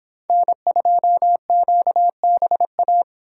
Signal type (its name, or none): Morse code